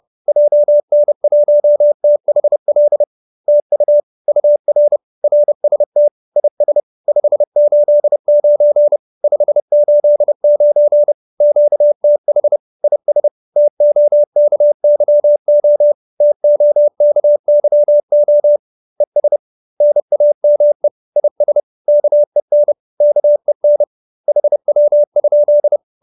Morse code, heard as a keyed tone, about 30 wpm, 590 Hz. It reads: JN1THL TU UR RST IS 589 589 QTH IS TOKYO TOKYO ES NAME IS KEN KEN HW?